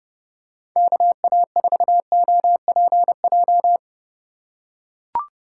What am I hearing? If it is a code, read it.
Morse code, heard as a keyed tone, about 32 wpm, 700 Hz. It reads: KA4OPJ
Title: USA Call Signs - 3x Character Spacing - 30wpm